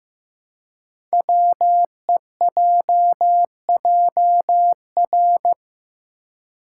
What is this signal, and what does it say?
Morse code, heard as a keyed tone, about 15 wpm, 700 Hz. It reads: WEJJR
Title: Code Groups - 5 Letters - 15wpm